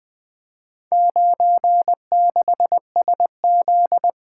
Morse code, heard as a keyed tone, about 20 wpm, 700 Hz. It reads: 96SZ
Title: Four Letter-Number Combinations - 20wpm